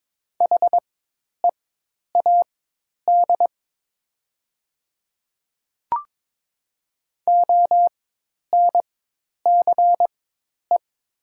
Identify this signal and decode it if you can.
Morse code, heard as a keyed tone, about 22 wpm, spaced as for 10 wpm, 700 Hz. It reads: HEAD ONCE
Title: 4x Character Spacing - Top 300 Words - 22wpm